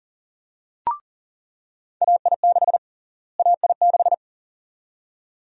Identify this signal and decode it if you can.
Morse code, heard as a keyed tone, about 40 wpm, 700 Hz. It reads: AI6 AI6
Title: USA Call Sign Prefixes - Repeated 2x - 40wpm